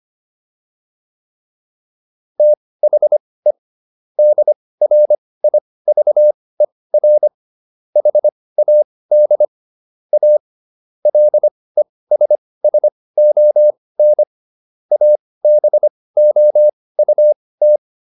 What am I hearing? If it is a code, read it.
Morse code, heard as a keyed tone, about 25 wpm, spaced as for 18 wpm, 600 Hz. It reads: THE DRIVER HAD A LESSON ABOUT